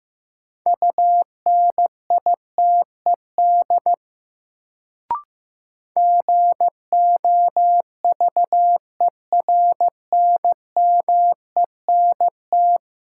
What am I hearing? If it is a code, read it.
Morse code, heard as a keyed tone, about 16 wpm, 700 Hz. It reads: UNITED GOVERNMENT
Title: Sets of 2 Words - New Word Drill - 15wpm